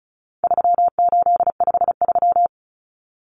Morse code, heard as a keyed tone, about 34 wpm, 700 Hz. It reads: 3853
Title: Common Cut Numbers - 4-Digit 35wpm